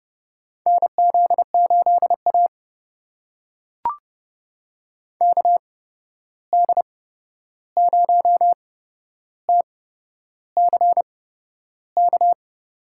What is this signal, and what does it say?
Morse code, heard as a keyed tone, about 32 wpm, spaced as for 11 wpm, 700 Hz. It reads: NZ8A KD0TCK
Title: USA Call Signs - 8x Character Spacing - 30wpm